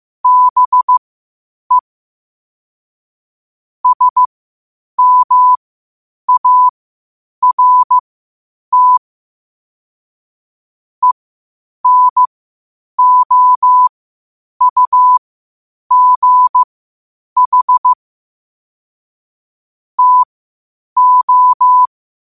Morse code, heard as a keyed tone, about 15 wpm, spaced as for 8 wpm, 1000 Hz. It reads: BE SMART ENOUGH TO
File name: qotdfiles / 8wpm